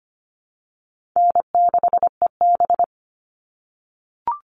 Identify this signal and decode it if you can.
Morse code, heard as a keyed tone, about 25 wpm, 700 Hz. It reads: N6EB
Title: All Cut Numbers - 4-Digit 25wpm